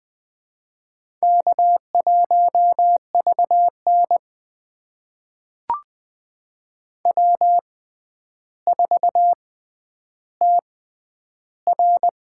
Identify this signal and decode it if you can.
Morse code, heard as a keyed tone, about 20 wpm, spaced as for 10 wpm, 700 Hz. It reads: K1VN W4TR
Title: USA Call Signs - 4 Characters - 6x Character Spacing - 20wpm